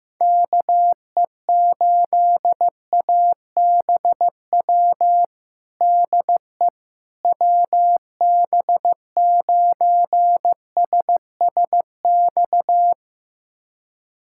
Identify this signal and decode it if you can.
Morse code, heard as a keyed tone, about 15 wpm, 700 Hz. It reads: KE8ABW DE WB9SSX
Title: USA Callsign DE Callsign - 6 Characters - 15wpm